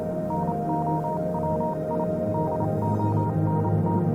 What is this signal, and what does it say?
Morse code, heard as a keyed tone, about 28 wpm, spaced as for 25 wpm, 940 Hz, with noise. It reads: NOW I DONT
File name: Vylet Pony / aura / do you remember the song she sang (morse code version)